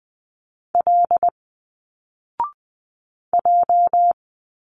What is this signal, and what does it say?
Morse code, heard as a keyed tone, about 20 wpm, 700 Hz. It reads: L J
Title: Single Letters 20wpm